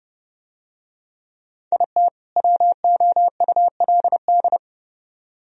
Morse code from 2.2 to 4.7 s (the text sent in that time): WOULD